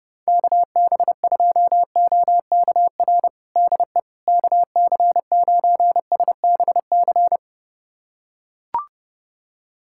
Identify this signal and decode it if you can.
Morse code, heard as a keyed tone, about 30 wpm, 700 Hz. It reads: KB2OKR DE KC9SBC
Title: USA Callsign DE Callsign - 6 Characters - 30wpm